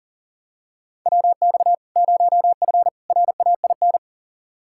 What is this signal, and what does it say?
Morse code, heard as a keyed tone, about 40 wpm, 700 Hz. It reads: WX 0F RAIN
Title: Practice QSO - Exchange 2 - Info - 40wpm